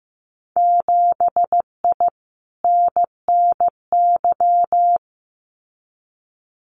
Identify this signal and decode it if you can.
Morse code, heard as a keyed tone, about 15 wpm, 700 Hz. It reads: 7I NNY
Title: Winter Field Day Exchanges Encore 15wpm